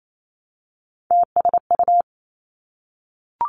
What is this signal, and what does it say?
Morse code, heard as a keyed tone, about 28 wpm, 700 Hz. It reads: TSU
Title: Three Challenging Letter-Number Combinations 28wpm